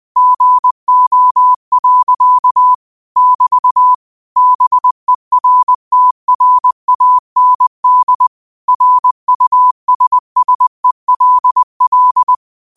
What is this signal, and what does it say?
Morse code, heard as a keyed tone, about 20 wpm, 1000 Hz. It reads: GO. <BT> BERTRAND RUSSELL
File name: qotdfiles / 20wpm